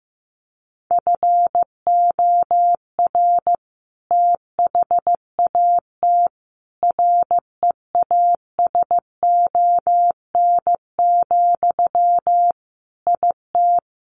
Morse code, heard as a keyed tone, about 15 wpm, 700 Hz. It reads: FOR THAT REASON, IT
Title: Sentences from Top 1500 Words 15wpm